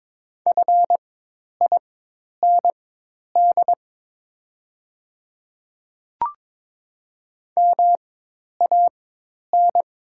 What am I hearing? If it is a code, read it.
Morse code, heard as a keyed tone, about 22 wpm, spaced as for 10 wpm, 700 Hz. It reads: FIND MAN
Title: 4x Character Spacing - Top 200 Words - 22wpm